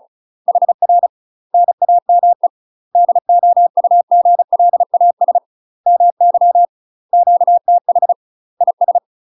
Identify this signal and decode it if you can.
Morse code, heard as a keyed tone, about 35 wpm, 700 Hz. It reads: HR NAME DOUGLAS MY QTH IS
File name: UK MASTER QSOs TRAINING / 35WPM / UK MASTER QSOs 0033